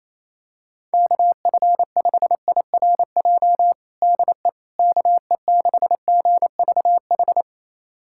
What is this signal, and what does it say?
Morse code, heard as a keyed tone, about 28 wpm, 700 Hz. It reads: KF5IRJ DE KE6GVH
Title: USA Callsign DE Callsign - 6 Characters - 28wpm